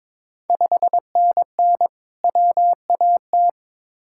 Morse code, heard as a keyed tone, about 22 wpm, 700 Hz. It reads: 5NN WAT
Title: Ontario QSO Party Exchange 22wpm